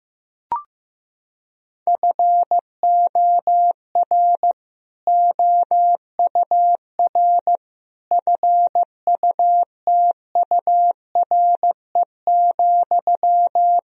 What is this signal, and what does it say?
Morse code, heard as a keyed tone, about 15 wpm, 700 Hz. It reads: FOR OUR FUTURE,
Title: Sentences from Top 1000 Words - Review Part 3 - 15wpm